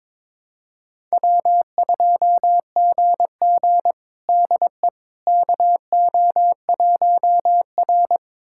Morse code, heard as a keyed tone, about 22 wpm, 700 Hz. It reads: W2GG DE KO1R
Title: USA Callsign DE Callsign - 4 Characters - 22wpm